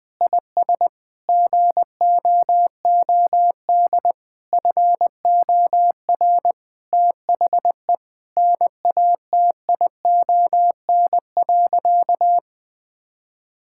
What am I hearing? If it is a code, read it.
Morse code, heard as a keyed tone, about 20 wpm, 700 Hz. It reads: IS GOOD FOR THE NATION.
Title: Sentences with 4-Word Phrases - 20wpm